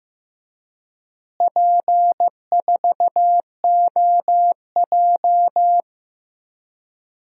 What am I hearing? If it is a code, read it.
Morse code, heard as a keyed tone, about 15 wpm, 700 Hz. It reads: P4OJ
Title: DXCC-100 Call Signs - 4 Characters - Repeated 3x - 15wpm